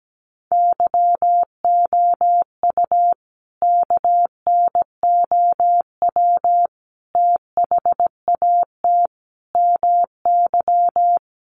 Morse code, heard as a keyed tone, about 17 wpm, 700 Hz. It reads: YOU KNOW THAT MY
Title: Sentences from Top 1400 Words 17wpm